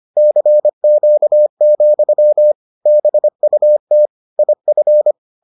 Morse code, heard as a keyed tone, about 25 wpm, 600 Hz. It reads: CQ, BUT IF